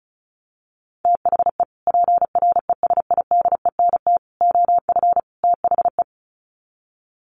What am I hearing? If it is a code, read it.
Morse code, heard as a keyed tone, about 34 wpm, 700 Hz. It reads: THE PRESIDENT OF THE
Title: Sets of 4 Words 35wpm